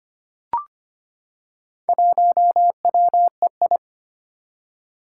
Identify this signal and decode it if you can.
Morse code, heard as a keyed tone, about 25 wpm, 700 Hz. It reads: 1WEI
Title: Four Challenging Letter-Number Combinations 25wpm